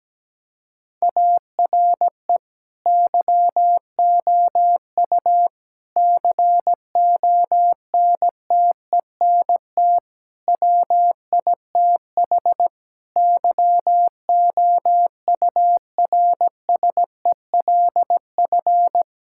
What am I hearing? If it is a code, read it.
Morse code, heard as a keyed tone, about 17 wpm, 700 Hz. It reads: ARE YOU CONTENT WITH YOURSELF
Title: Sentences from Top 1000 Words - Review Part 6 - 17wpm